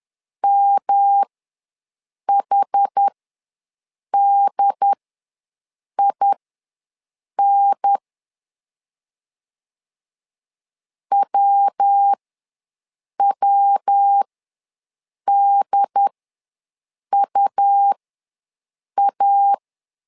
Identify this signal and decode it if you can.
Morse code, heard as a keyed tone, about 10 wpm, spaced as for 6 wpm, 800 Hz. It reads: MHDIN WWDUA
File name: LEZIONI / 3 MWRO / 3 MWRO ws 10wpm 2310